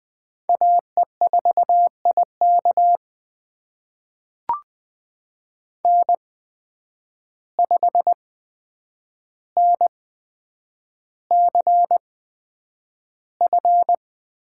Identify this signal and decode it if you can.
Morse code, heard as a keyed tone, about 20 wpm, spaced as for 9 wpm, 700 Hz. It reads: AE4IK N5NCF
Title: USA Call Signs - 5 Characters - 8x Character Spacing - 20wpm